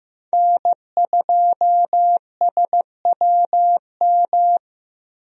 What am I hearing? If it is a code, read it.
Morse code, heard as a keyed tone, about 15 wpm, 700 Hz. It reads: N2SWM
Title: USA Call Signs - 5 Characters - 15wpm